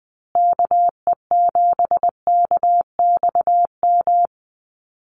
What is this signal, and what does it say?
Morse code, heard as a keyed tone, about 20 wpm, 700 Hz. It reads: KE7KXM
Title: USA Call Signs Repeated 2x 20wpm